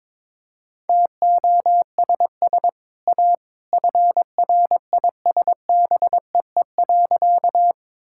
Morse code, heard as a keyed tone, about 22 wpm, 700 Hz. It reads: TOSS A FRISBEE.
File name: practice / 22wpm / Easy Sentences with Hints - 22wpm